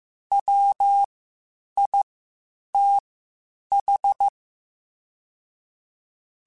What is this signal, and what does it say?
Morse code, heard as a keyed tone, about 15 wpm, spaced as for 8 wpm, 800 Hz. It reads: WITH
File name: Spaced Characters / Spaced Characters-mp3s / Weather 3 17 sp @13